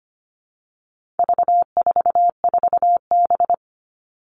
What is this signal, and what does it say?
Morse code, heard as a keyed tone, about 25 wpm, 700 Hz. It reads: V44B
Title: All Cut Numbers - 4-Digit 25wpm